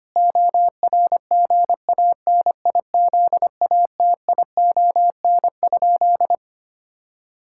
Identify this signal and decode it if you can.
Morse code, heard as a keyed tone, about 25 wpm, 700 Hz. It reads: ORGANIZATION?
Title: Sentences from Top 1000 Words - Review Part 5 - 25wpm